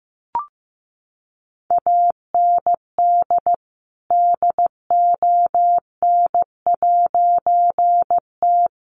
Morse code, heard as a keyed tone, about 14 wpm, 700 Hz. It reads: AND DON'T
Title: A Sound Of Thunder 15wpm